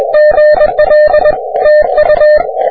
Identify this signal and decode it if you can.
Morse code, heard as a keyed tone, about 22 wpm, 600 Hz. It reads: GL TU
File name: wp / vc1w 30m cw